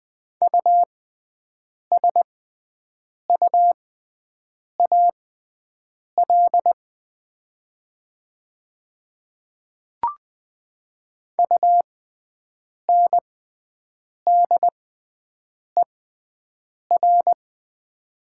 Morse code, heard as a keyed tone, about 20 wpm, spaced as for 7 wpm, 700 Hz. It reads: USUAL UNDER
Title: 6x Character Spacing - Top 500 Words - 20wpm